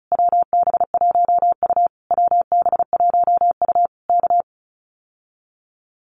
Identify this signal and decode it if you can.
Morse code, heard as a keyed tone, about 34 wpm, 700 Hz. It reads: WB1U WB1U K
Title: Common QSO Phrases Encore 35wpm